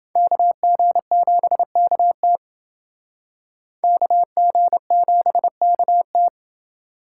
Morse code, heard as a keyed tone, about 30 wpm, 700 Hz. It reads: KG7KT KG7KT